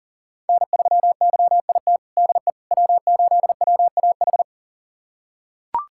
Morse code, heard as a keyed tone, about 40 wpm, 700 Hz. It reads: N3YIT DE W8WAH
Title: USA Callsign DE Callsign - 5 Characters - 40wpm